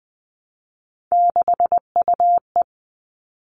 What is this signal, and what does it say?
Morse code, heard as a keyed tone, about 20 wpm, 700 Hz. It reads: 6UE
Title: All Cut Numbers - 3-Digit 20wpm